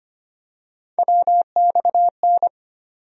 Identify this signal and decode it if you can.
Morse code, heard as a keyed tone, about 25 wpm, 700 Hz. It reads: WXN
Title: USA Call Sign Postfixes - 3 Letters 25wpm